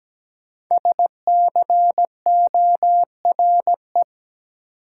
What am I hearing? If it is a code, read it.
Morse code, heard as a keyed tone, about 17 wpm, 700 Hz. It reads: SCORE
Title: Sentences from Top 1600 Words - New Word Drill - 17wpm